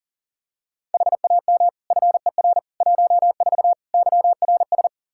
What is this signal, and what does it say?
Morse code, heard as a keyed tone, about 40 wpm, 700 Hz. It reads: HAM FER 14 YRS
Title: Practice QSO - Exchange 3 - More Info - 40wpm